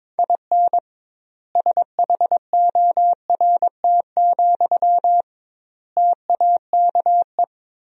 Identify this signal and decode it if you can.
Morse code, heard as a keyed tone, about 22 wpm, 700 Hz. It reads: IN SHORT, TAKE
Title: 2x Word-Spacing - Sentences from Top 400 Words - 22wpm